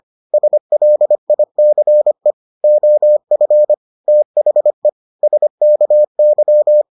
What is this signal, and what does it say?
Morse code, heard as a keyed tone, about 25 wpm, 600 Hz. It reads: SLICE OF THE SKY